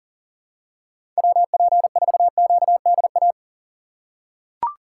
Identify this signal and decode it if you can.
Morse code, heard as a keyed tone, about 40 wpm, 700 Hz. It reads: WP4QDA
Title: USA Call Signs - 6x Character Spacing - 40wpm